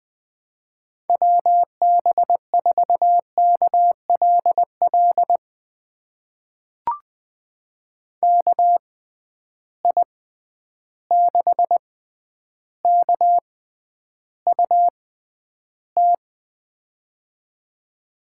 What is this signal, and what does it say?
Morse code, heard as a keyed tone, about 20 wpm, spaced as for 10 wpm, 700 Hz. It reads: WB4KLL KI6KUT
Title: USA Call Signs - 6 Characters - 6x Character Spacing - 20wpm